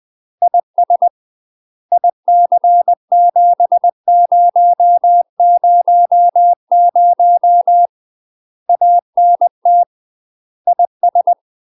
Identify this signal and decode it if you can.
Morse code, heard as a keyed tone, about 20 wpm, 700 Hz. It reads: IS IC7000 ANT IS
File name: UK MASTER QSOs TRAINING / 20WPM / UK MASTER QSOs 0048